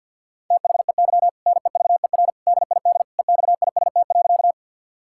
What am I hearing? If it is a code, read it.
Morse code, heard as a keyed tone, about 52 wpm, 700 Hz. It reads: THEY NEVER DID EXIST.
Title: Sentences from Top 1000 Words - Review Part 5 - 50wpm